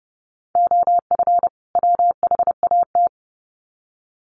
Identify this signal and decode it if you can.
Morse code, heard as a keyed tone, about 30 wpm, 700 Hz. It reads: OF WHAT
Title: Sets of 2 Words 30wpm